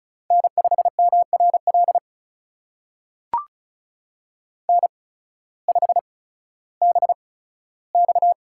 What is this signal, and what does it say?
Morse code, heard as a keyed tone, about 36 wpm, spaced as for 16 wpm, 700 Hz. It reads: N5MRL N5BX
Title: USA Call Signs - 8x Character Spacing - 35wpm